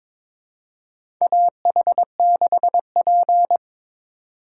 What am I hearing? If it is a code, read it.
Morse code, heard as a keyed tone, about 22 wpm, 700 Hz. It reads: AH6P
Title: USA Call Signs - 2x Character Spacing - 22wpm